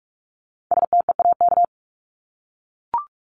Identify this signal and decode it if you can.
Morse code, heard as a keyed tone, about 44 wpm, 700 Hz. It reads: STEAK
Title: Top 1251-1500 Words 45wpm